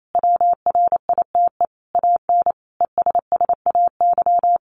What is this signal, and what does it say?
Morse code, heard as a keyed tone, about 28 wpm, 700 Hz. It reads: WRITE AN ESSAY
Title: Sentences from Top 1500 Words 28wpm